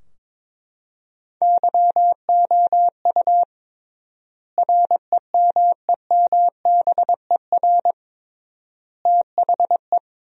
Morse code, heard as a keyed tone, about 21 wpm, 700 Hz. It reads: YOU REMEMBER THE
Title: 3x Word-Spacing - Sentences from Top 400 Words - 22wpm